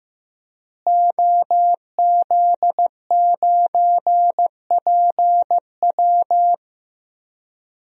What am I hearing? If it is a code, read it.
Morse code, heard as a keyed tone, about 16 wpm, 700 Hz. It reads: OZ9PW
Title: DXCC-100 Call Signs - 5 Characters - Repeated 2x - 15wpm